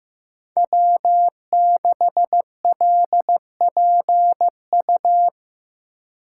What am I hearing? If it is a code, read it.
Morse code, heard as a keyed tone, about 15 wpm, 700 Hz. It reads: W6LPU
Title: USA Call Signs - 5 Characters - 3x Character Spacing - 15wpm